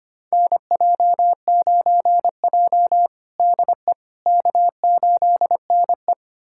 Morse code, heard as a keyed tone, about 25 wpm, 700 Hz. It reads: NJ9J DE K8NE
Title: USA Callsign DE Callsign - 4 Characters - 25wpm